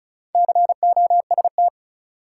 Morse code, heard as a keyed tone, about 36 wpm, 700 Hz. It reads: COST
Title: Sets of 3 Words - New Word Drill - 35wpm